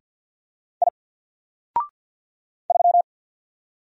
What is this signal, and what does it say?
Morse code, heard as a keyed tone, about 50 wpm, 700 Hz. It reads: I 3
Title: Single Letter-Number 50wpm